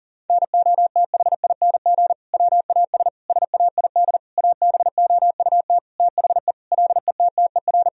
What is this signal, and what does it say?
Morse code, heard as a keyed tone, about 40 wpm, 700 Hz. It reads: NOTHING WAS SAID ABOUT THE LETTER
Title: Sentences from Top 1000 Words - Review Part 2 - 40wpm